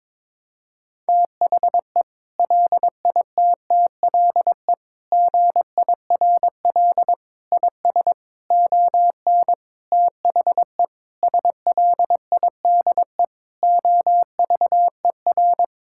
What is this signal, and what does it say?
Morse code, heard as a keyed tone, about 22 wpm, 700 Hz. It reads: THE LITTLE GIRL IS ON THE SLIDE OVER
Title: Sentences from Top 1600 Words 22wpm